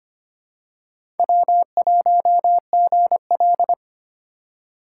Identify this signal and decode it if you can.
Morse code, heard as a keyed tone, about 25 wpm, 700 Hz. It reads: W1GL
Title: USA Call Signs - 4 Characters - 2x Character Spacing - 25wpm